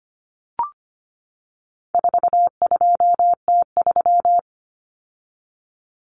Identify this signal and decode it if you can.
Morse code, heard as a keyed tone, about 25 wpm, 700 Hz. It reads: 42T3
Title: Common Cut Numbers - 4-Digit 25wpm